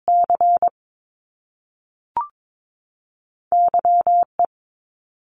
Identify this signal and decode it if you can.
Morse code, heard as a keyed tone, about 22 wpm, 700 Hz. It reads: C Y E
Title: Top Two Letter Combinations in Words 22wpm